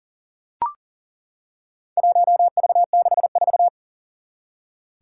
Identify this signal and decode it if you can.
Morse code, heard as a keyed tone, about 40 wpm, 700 Hz. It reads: 1V64
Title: Four Challenging Letter-Number Combinations 40wpm